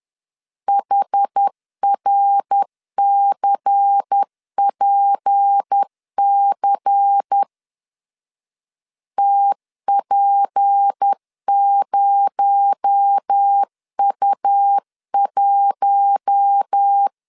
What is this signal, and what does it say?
Morse code, heard as a keyed tone, about 10 wpm, 800 Hz. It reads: HRCPC TP0U1
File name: CW / DET / 1 6 1500C S B 10wpm 4137